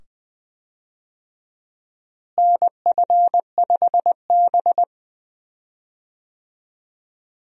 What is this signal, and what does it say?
Morse code, heard as a keyed tone, about 20 wpm, 700 Hz. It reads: NF5B
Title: USA Call Signs - 4 Characters - Repeated 2x - 20wpm